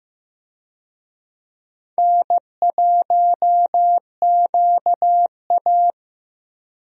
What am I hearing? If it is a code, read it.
Morse code, heard as a keyed tone, about 15 wpm, 700 Hz. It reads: N1QA
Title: USA Call Signs - 4 Characters - Repeated 2x - 15wpm